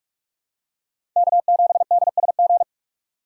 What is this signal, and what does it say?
Morse code, heard as a keyed tone, about 46 wpm, 700 Hz. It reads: K7DSG
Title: USA Call Signs - 3x Character Spacing - 45wpm